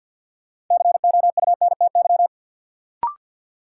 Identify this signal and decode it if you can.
Morse code, heard as a keyed tone, about 52 wpm, 700 Hz. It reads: COUNTY
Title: Sentences from Top 1200 Words - New Word Drill - 50wpm